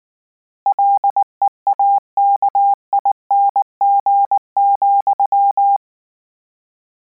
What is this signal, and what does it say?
Morse code, heard as a keyed tone, about 19 wpm, 800 Hz. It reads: LEAKING,
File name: Combinations / Combinations-mp3s / Combinations@18